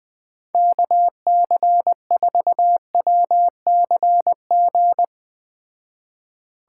Morse code, heard as a keyed tone, about 21 wpm, 700 Hz. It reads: KC4WCG